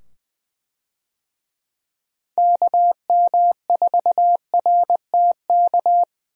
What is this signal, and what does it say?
Morse code, heard as a keyed tone, about 20 wpm, 700 Hz. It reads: KM4RTK
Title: USA Call Signs - 6 Characters - 20wpm